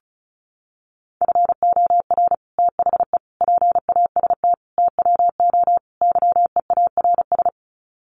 Morse code, heard as a keyed tone, about 34 wpm, 700 Hz. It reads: FOR THE PAST TWO YEARS
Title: Sets of 5 Words Encore2 35wpm